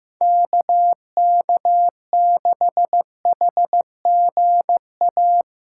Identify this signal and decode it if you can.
Morse code, heard as a keyed tone, about 15 wpm, 700 Hz. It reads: KK6HGA
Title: USA Call Signs - 6 Characters - 3x Character Spacing - 15wpm